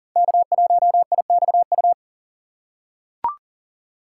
Morse code, heard as a keyed tone, about 40 wpm, 700 Hz. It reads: K1IXU
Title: USA Call Signs - 3x Character Spacing - 40wpm